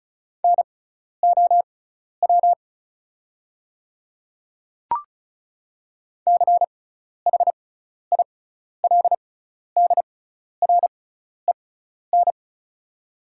Morse code, heard as a keyed tone, about 36 wpm, spaced as for 12 wpm, 690 Hz. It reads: NOW CHILDREN
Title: 6x Character Spacing - Top 300 Words - 35wpm